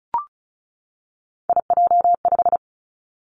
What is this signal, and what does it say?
Morse code, heard as a keyed tone, about 36 wpm, 700 Hz. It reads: IJ5
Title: Three Challenging Letter-Number Combinations 35wpm